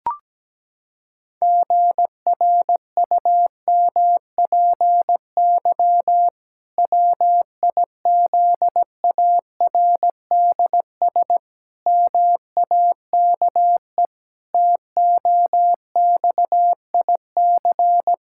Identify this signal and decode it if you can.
Morse code, heard as a keyed tone, about 17 wpm, 700 Hz. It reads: GRUMPY WIZARDS MAKE TOXIC
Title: Pangram Challenge v2 - 17wpm